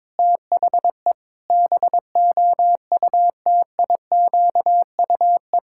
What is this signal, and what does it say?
Morse code, heard as a keyed tone, about 22 wpm, 700 Hz. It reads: THE BOUTIQUE
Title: Pangram Challenge v2 - 22wpm